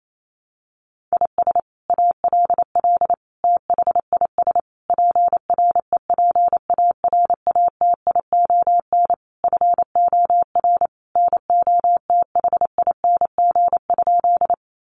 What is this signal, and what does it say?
Morse code, heard as a keyed tone, about 28 wpm, 700 Hz. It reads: IS ALL THIS PREPARATION FOR NOTHING?
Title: Sentences from Top 1400 Words 28wpm